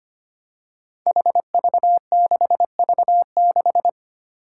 Morse code, heard as a keyed tone, about 25 wpm, 700 Hz. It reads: HV6V6
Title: Five Challenging Letter-Number Combinations 25wpm